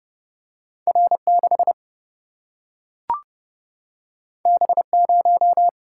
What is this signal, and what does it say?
Morse code, heard as a keyed tone, about 30 wpm, 700 Hz. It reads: R6 B0
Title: Single-Letter Single-Number 30wpm